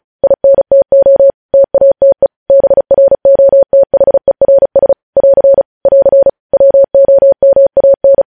Morse code, heard as a keyed tone, about 36 wpm, 550 Hz. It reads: INTO TATE BROTHERS <AR> <AR> WOMAN
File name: podcast / News headlines-1x-35x35